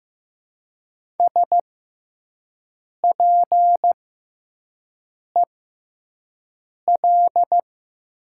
Morse code, heard as a keyed tone, about 15 wpm, spaced as for 5 wpm, 700 Hz. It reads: SPEL
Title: 6x Character Spacing - Top 200 Words - 15wpm